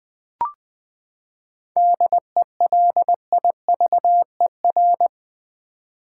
Morse code, heard as a keyed tone, about 20 wpm, 700 Hz. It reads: DELIVER
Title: Seven Letter Words - 20wpm